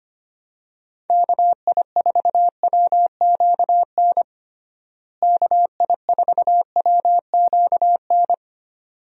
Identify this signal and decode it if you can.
Morse code, heard as a keyed tone, about 25 wpm, 700 Hz. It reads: KI4WQN KI4WQN
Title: USA Call Signs - 6 Characters - Repeated 2x - 25wpm